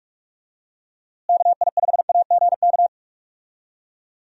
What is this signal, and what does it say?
Morse code, heard as a keyed tone, about 43 wpm, 700 Hz. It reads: KI5AGK
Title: USA Call Signs - 6 Characters - Repeated 2x - 45wpm